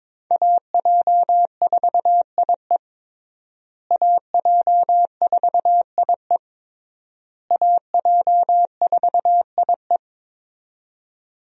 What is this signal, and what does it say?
Morse code, heard as a keyed tone, about 22 wpm, 700 Hz. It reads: AJ4IE AJ4IE AJ4IE
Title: USA Call Signs - 5 Characters - Repeated 3x - 22wpm